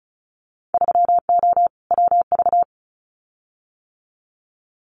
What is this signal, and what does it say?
Morse code, heard as a keyed tone, about 36 wpm, 700 Hz. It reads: 3O WV
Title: Winter Field Day Exchanges 35wpm